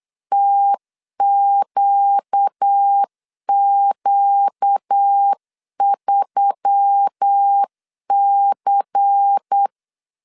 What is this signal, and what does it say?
Morse code, heard as a keyed tone, about 9 wpm, 800 Hz. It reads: TQQ3C
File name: CW / DET / 1 6 1500C S C 8wpm 5219